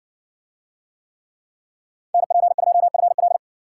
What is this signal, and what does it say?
Morse code, heard as a keyed tone, about 60 wpm, 700 Hz. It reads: NP2FL